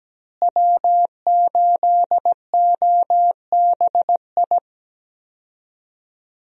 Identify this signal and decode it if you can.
Morse code, heard as a keyed tone, about 17 wpm, 700 Hz. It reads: W8OBI
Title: USA Call Signs - 5 Characters - Repeated 2x - 17wpm